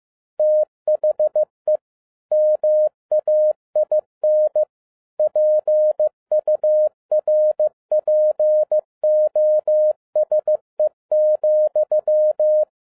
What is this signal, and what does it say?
Morse code, heard as a keyed tone, about 15 wpm, 610 Hz. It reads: THE MAIN PURPOSE,
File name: 2011-07-mp3 / n12 T8XO 15wpm